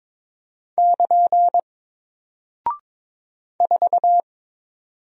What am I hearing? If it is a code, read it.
Morse code, heard as a keyed tone, about 22 wpm, 700 Hz. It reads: <KN> 4
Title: Single Character (Letter:Number:Punctuation:Prosign) 22wpm